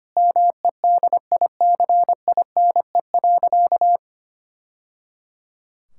Morse code, heard as a keyed tone, about 25 wpm, 700 Hz. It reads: MEDICINE.
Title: Sentences from Top 1000 Words - Review Part 6 - 25wpm